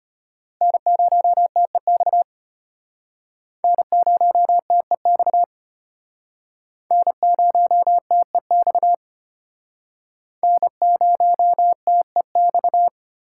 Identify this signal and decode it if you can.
Morse code, heard as a keyed tone, about 32 wpm, 700 Hz. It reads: N0TEX N0TEX N0TEX N0TEX